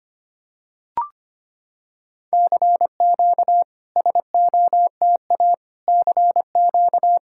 Morse code, heard as a keyed tone, about 25 wpm, 690 Hz. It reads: CQ SOTA CQ
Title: SOTA QSO Practice 25wpm